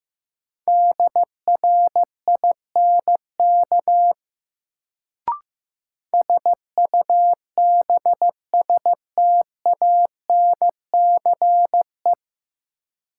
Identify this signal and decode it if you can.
Morse code, heard as a keyed tone, about 15 wpm, 700 Hz. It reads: DRINK SUBSTANCE
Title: Sentences from Top 1500 Words - New Word Drill Alt - 15wpm